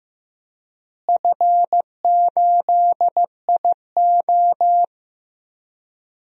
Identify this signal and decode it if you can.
Morse code, heard as a keyed tone, about 15 wpm, 700 Hz. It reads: F8IO
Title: DXCC-100 Call Signs - 4 Characters - Repeated 3x - 15wpm